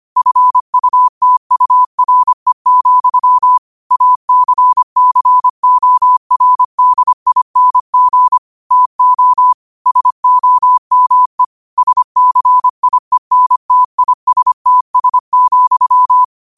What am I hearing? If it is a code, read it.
Morse code, heard as a keyed tone, about 25 wpm, 1000 Hz. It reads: FUTURE, ACCORDING TO SOME SCIENTISTS,
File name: qotdfiles / 25wpm